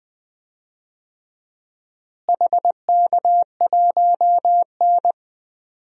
Morse code, heard as a keyed tone, about 20 wpm, 700 Hz. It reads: HK1N
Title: DXCC-100 Call Signs - 4 Characters - Repeated 2x - 20wpm